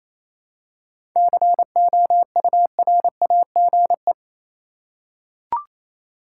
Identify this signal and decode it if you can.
Morse code, heard as a keyed tone, about 28 wpm, 700 Hz. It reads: COURAGE
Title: Seven Letter Words - 28wpm